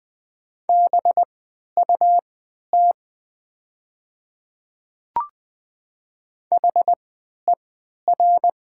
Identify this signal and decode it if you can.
Morse code, heard as a keyed tone, about 20 wpm, spaced as for 11 wpm, 700 Hz. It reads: BUT HER
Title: 3x Character Spacing - Top 100 Words - 20wpm